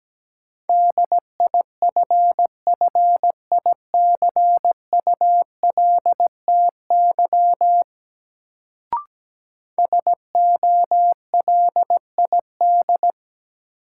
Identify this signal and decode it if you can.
Morse code, heard as a keyed tone, about 17 wpm, 700 Hz. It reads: DIFFICULTY SOLID
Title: Sentences from Top 1500 Words - New Word Drill - 17wpm